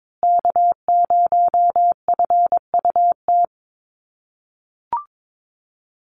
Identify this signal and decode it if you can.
Morse code, heard as a keyed tone, about 22 wpm, 700 Hz. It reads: K0FUT
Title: USA Call Signs Repeated 3x 22wpm